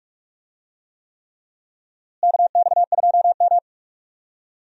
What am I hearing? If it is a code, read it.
Morse code, heard as a keyed tone, about 41 wpm, 700 Hz. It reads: KX2M
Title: USA Call Signs - 4 Characters - Repeated 3x - 45wpm